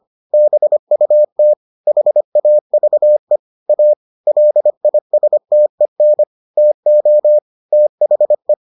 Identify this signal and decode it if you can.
Morse code, heard as a keyed tone, about 25 wpm, 600 Hz. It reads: BUT HAVE A LISTEN TO THE